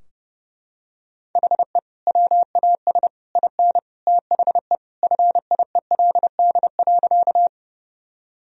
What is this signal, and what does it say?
Morse code, heard as a keyed tone, about 32 wpm, 700 Hz. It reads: HE WAS IN THE FIELD.
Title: Sentences from Top 1000 Words - Review Part 2 - 30wpm